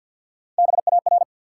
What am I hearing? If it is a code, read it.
Morse code, heard as a keyed tone, about 51 wpm, 700 Hz. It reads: BAR
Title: Three Letter Words - 50wpm